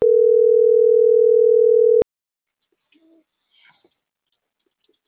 Ringback tone (North America), 440 + 480 Hz.